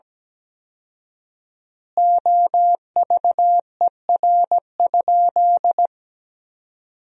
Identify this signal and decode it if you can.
Morse code, heard as a keyed tone, about 17 wpm, 700 Hz. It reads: OVER?
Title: 4x Word-Spacing - Sentences from Top 100 Words - 17wpm